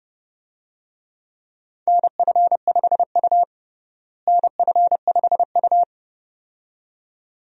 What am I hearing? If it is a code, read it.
Morse code, heard as a keyed tone, about 31 wpm, 700 Hz. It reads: NF5U NF5U
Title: USA Call Signs - 4 Characters - Repeated 2x - 30wpm